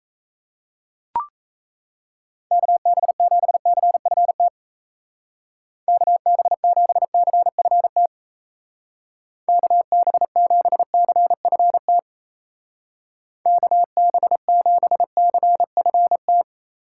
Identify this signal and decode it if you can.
Morse code, heard as a keyed tone, about 34 wpm, 700 Hz. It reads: KB7CFT KB7CFT KB7CFT KB7CFT
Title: Speed Racing - USA 2x3 Call Signs 28wpm